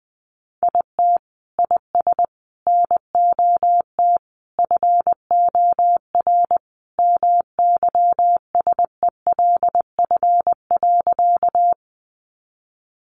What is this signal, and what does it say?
Morse code, heard as a keyed tone, about 20 wpm, 700 Hz. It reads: IT IS NOT FOR MYSELF.
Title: Sentences from Top 1300 Words 20wpm